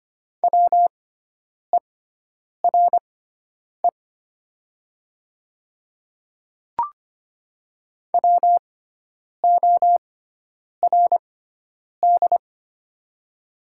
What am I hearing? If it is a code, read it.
Morse code, heard as a keyed tone, about 25 wpm, spaced as for 9 wpm, 700 Hz. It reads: WERE WORD
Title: 6x Character Spacing - Top 100 Words - 25wpm